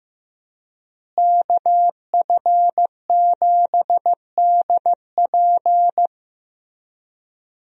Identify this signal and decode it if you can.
Morse code, heard as a keyed tone, about 15 wpm, 700 Hz. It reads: KF7DP
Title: USA Call Signs - 5 Characters - Repeated 2x - 15wpm